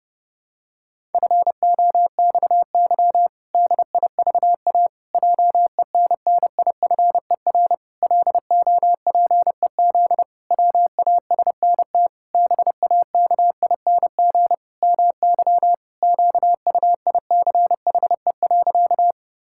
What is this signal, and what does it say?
Morse code, heard as a keyed tone, about 30 wpm, 700 Hz. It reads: FOXY DIVA JENNIFER LOPEZ WASNT BAKING MY QUICHE.
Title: Pangram Challenge v2 - 30wpm